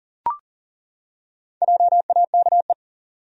Morse code, heard as a keyed tone, about 40 wpm, 700 Hz. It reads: JAKE
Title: USA Common Names - Max 4 Letters - 40wpm